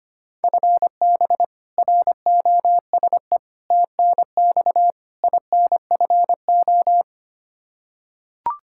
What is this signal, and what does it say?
Morse code, heard as a keyed tone, about 25 wpm, 700 Hz. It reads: FB ROSE TNX INFO
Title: Practice QSO - Exchange 3 - More Info - 25wpm